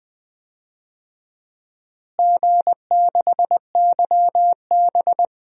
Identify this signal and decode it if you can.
Morse code, heard as a keyed tone, about 20 wpm, 700 Hz. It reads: G6YB
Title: DXCC-100 Call Signs - 4 Characters - Repeated 3x - 20wpm